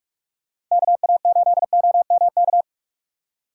Morse code, heard as a keyed tone, about 45 wpm, 700 Hz. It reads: KA8OMK